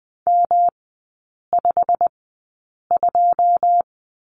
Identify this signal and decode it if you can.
Morse code, heard as a keyed tone, about 20 wpm, spaced as for 8 wpm, 700 Hz. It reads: M52
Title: 107 Lesson 18e - Letter m - Three Characters 20wpm